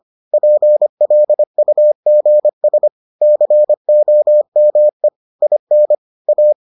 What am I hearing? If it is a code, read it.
Morse code, heard as a keyed tone, about 25 wpm, 600 Hz. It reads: PLUGS COME IN A